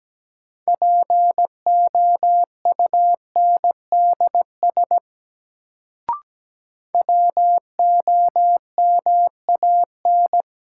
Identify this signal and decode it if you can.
Morse code, heard as a keyed tone, about 17 wpm, 700 Hz. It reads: POUNDS WOMAN
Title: Sentences from Top 1000 Words - Review Part 2 - New Word Drill - 17wpm